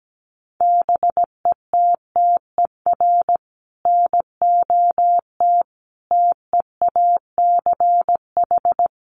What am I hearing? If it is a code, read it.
Morse code, heard as a keyed tone, about 17 wpm, 700 Hz. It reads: BETTER NOT TEACH
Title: Sentences from Top 600 Words 17wpm